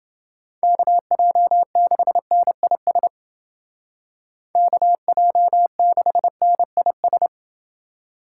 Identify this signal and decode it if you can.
Morse code, heard as a keyed tone, about 28 wpm, 700 Hz. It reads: KJ6NIS KJ6NIS